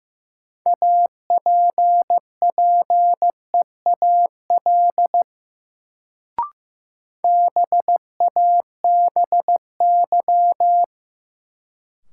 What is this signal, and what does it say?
Morse code, heard as a keyed tone, about 15 wpm, 700 Hz. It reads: APPEAL BABY
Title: Sentences from Top 1300 Words - New Word Drill Alt - 15wpm